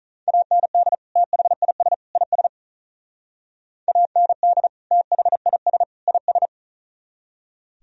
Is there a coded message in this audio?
Yes: Morse code from 0.3 to 6.5 s, heard as a keyed tone, about 44 wpm, 700 Hz. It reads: AND THIS IS AND THIS IS